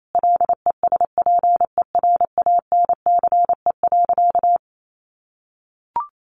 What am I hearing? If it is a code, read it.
Morse code, heard as a keyed tone, about 28 wpm, 700 Hz. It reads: LESPERANCE.
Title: A Sound Of Thunder 28wpm